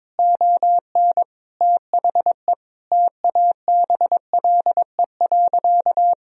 Morse code, heard as a keyed tone, about 22 wpm, 700 Hz. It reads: ON THE TABLE.
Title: Sentences from Top 1000 Words - Review Part 4 - 22wpm